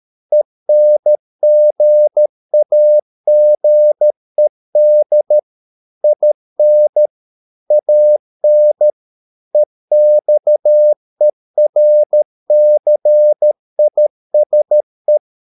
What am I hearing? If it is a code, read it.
Morse code, heard as a keyed tone, about 13 wpm, 600 Hz. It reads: ENGAGED IN AN EXERCISE
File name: Never 0013wpm